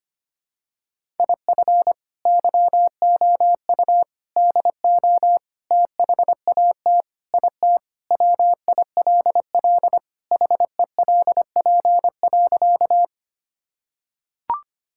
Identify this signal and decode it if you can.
Morse code, heard as a keyed tone, about 25 wpm, 700 Hz. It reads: IF YOU DO THAT IT WILL HELP.
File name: practice / 25wpm / Sentences with 3-Word Phrases - 25wpm